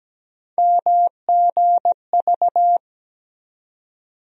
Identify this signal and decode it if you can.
Morse code, heard as a keyed tone, about 17 wpm, 700 Hz. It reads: MGV
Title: USA Call Sign Postfixes - 3 Letters 17wpm